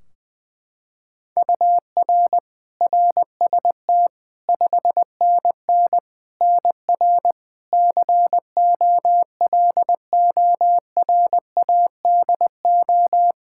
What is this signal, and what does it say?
Morse code, heard as a keyed tone, about 20 wpm, 700 Hz. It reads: UR RST 5NN NR COLORADO
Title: Practice QSO - Exchange 1 - Beginning - 20wpm